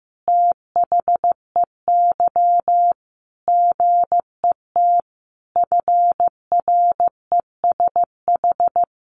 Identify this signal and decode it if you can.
Morse code, heard as a keyed tone, about 15 wpm, 700 Hz. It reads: THEY GET FRESH
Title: Sentences from Top 1200 Words 15wpm